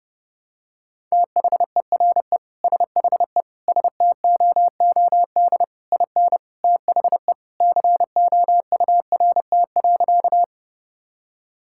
Morse code, heard as a keyed tone, about 30 wpm, 700 Hz. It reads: THERE SHE STOOD IN THE COURT.
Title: Sentences from Top 1600 Words 30wpm